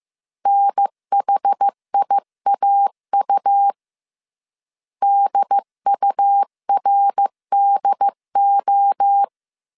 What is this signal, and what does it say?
Morse code, heard as a keyed tone, about 15 wpm, 800 Hz. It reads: NHIAU DURDO
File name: CW / DET / 1 3 1500C S A 14wpm 2447